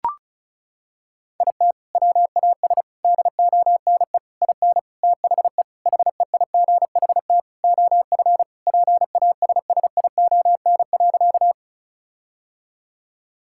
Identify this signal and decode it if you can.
Morse code, heard as a keyed tone, about 34 wpm, 700 Hz. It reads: IT WAS DONE IN THE HEIGHT OF PASSION.
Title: Sentences from Top 1600 Words 35wpm